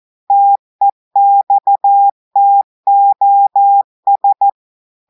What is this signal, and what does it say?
Morse code, heard as a keyed tone, about 14 wpm, 800 Hz. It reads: TEXTOS
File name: audios / portugais / portugais texte en clair 1 vitesse 14 mots minutes